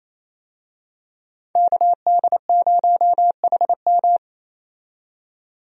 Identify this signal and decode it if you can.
Morse code, heard as a keyed tone, about 28 wpm, 700 Hz. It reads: KD0HM
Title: USA Call Signs - 5 Characters - 28wpm